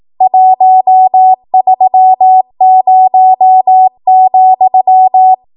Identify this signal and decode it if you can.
Morse code, heard as a keyed tone, about 18 wpm, 750 Hz. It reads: 130,
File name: Morse / 18WPM